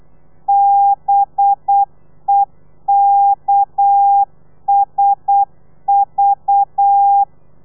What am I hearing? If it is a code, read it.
Morse code, heard as a keyed tone, about 8 wpm, 800 Hz. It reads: BEKSV
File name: telegrafia / i2tfj / Lez4 35